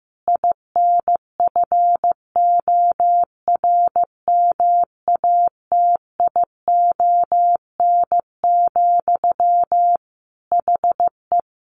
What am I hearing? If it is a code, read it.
Morse code, heard as a keyed tone, about 15 wpm, 700 Hz. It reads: INFORMATION, HE
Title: Sentences from Top 600 Words 15wpm